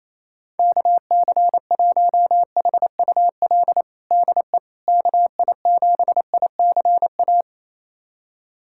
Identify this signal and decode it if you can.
Morse code, heard as a keyed tone, about 28 wpm, 700 Hz. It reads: KC1HUL DE KI7ICA
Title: USA Callsign DE Callsign - 6 Characters - 28wpm